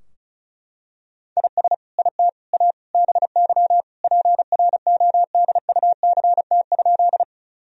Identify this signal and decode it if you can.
Morse code, heard as a keyed tone, about 36 wpm, 700 Hz. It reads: IS IT A BY PRODUCT?
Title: Sentences from Top 1000 Words - Review Part 2 - 35wpm